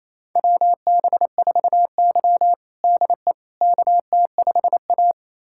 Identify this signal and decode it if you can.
Morse code, heard as a keyed tone, about 28 wpm, 700 Hz. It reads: WB4Y DE KT5A
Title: USA Callsign DE Callsign - 4 Characters - 28wpm